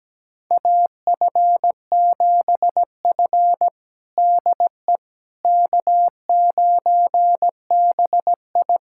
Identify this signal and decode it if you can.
Morse code, heard as a keyed tone, about 17 wpm, 700 Hz. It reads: AF7F DE K9BI
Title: USA Callsign DE Callsign - 4 Characters - 17wpm